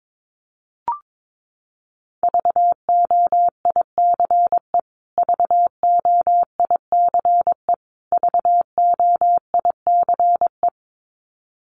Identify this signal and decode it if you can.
Morse code, heard as a keyed tone, about 22 wpm, 700 Hz. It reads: VOICE VOICE VOICE
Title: Top 500 Words Repeated 22wpm